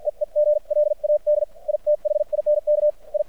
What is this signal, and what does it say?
Morse code, heard as a keyed tone, about 53 wpm, spaced as for 44 wpm, 600 Hz. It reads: EE GLAD I THINK I
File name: archived / CRSnet